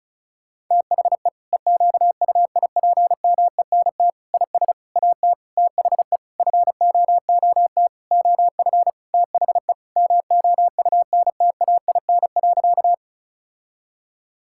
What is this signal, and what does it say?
Morse code, heard as a keyed tone, about 36 wpm, 700 Hz. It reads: THE EQUIPMENT IS AT THE FOOT OF THE MOUNTAIN.
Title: Sentences from Top 1000 Words - Review Part 5 - 35wpm